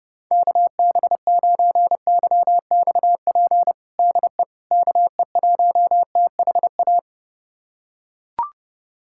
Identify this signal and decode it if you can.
Morse code, heard as a keyed tone, about 30 wpm, 700 Hz. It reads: KB9YXP DE KE1THA
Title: USA Callsign DE Callsign 30wpm